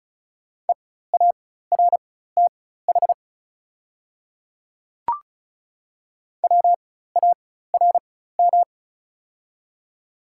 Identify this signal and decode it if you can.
Morse code, heard as a keyed tone, about 35 wpm, spaced as for 16 wpm, 700 Hz. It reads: EARTH WARM
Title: 4x Character Spacing - Top 500 Words - 35wpm